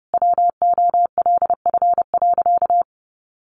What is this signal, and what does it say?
Morse code, heard as a keyed tone, about 30 wpm, 700 Hz. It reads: WOLF.
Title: Additional Easy Phrases 30wpm